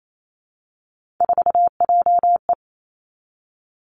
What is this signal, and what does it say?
Morse code, heard as a keyed tone, about 28 wpm, 700 Hz. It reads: 4JE
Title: Three Challenging Letter-Number Combinations 28wpm